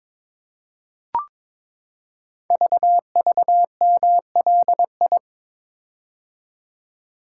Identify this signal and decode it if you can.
Morse code, heard as a keyed tone, about 22 wpm, 700 Hz. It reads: VVMLI
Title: Code Groups - 5 Letters - 22wpm